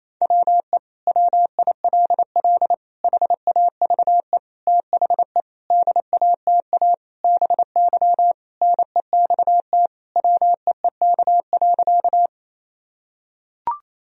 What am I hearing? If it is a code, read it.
Morse code, heard as a keyed tone, about 28 wpm, 700 Hz. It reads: WE WILL HAVE THE DATA BY NEXT WEEK.
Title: Sentences from Top 1000 Words - Review Part 3 - 28wpm